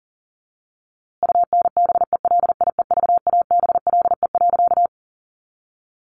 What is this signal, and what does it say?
Morse code, heard as a keyed tone, about 40 wpm, 700 Hz. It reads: UNBELIEVABLE.
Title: A Sound Of Thunder 40wpm